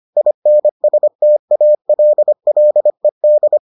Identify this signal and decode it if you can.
Morse code, heard as a keyed tone, about 25 wpm, 600 Hz. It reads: INSTALLED